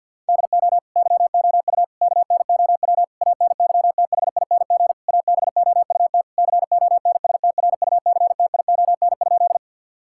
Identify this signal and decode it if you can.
Morse code, heard as a keyed tone, about 50 wpm, 700 Hz. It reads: DO YOU KNOW ANYTHING ABOUT CONSTRUCTION?
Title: Sentences from Top 1600 Words 50wpm